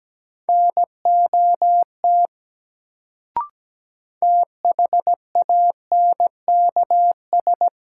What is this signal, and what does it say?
Morse code, heard as a keyed tone, about 17 wpm, 700 Hz. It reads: NOT THANKS
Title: Sentences from Top 600 Words - New Word Drill - 17wpm